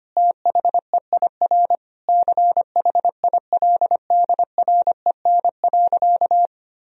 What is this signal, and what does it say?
Morse code, heard as a keyed tone, about 25 wpm, 700 Hz. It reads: THEIR CHILDREN.
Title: Sentences from Top 1000 Words - Review Part 3 - 25wpm